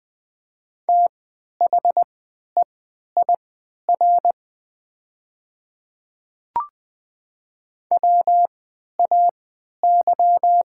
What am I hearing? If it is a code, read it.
Morse code, heard as a keyed tone, about 20 wpm, spaced as for 11 wpm, 700 Hz. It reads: THEIR WAY
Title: 3x Character Spacing - Top 100 Words - 20wpm